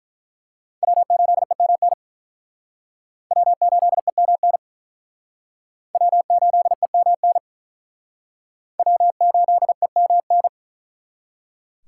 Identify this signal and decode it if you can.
Morse code, heard as a keyed tone, about 42 wpm, 700 Hz. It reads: W8EMN W8EMN W8EMN W8EMN
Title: Speed Racing - USA Call Signs 35wpm